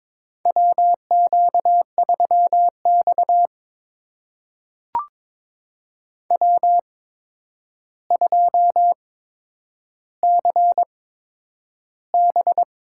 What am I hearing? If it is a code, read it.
Morse code, heard as a keyed tone, about 22 wpm, spaced as for 9 wpm, 700 Hz. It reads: WQ3X W2CB
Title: USA Call Signs - 4 Characters - 8x Character Spacing - 22wpm